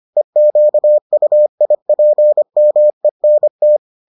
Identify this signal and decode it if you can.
Morse code, heard as a keyed tone, about 25 wpm, 600 Hz. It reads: EQUIPMENT